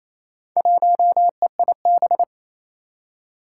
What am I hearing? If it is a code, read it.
Morse code, heard as a keyed tone, about 28 wpm, 700 Hz. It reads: 1EIB
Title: Four Challenging Letter-Number Combinations 28wpm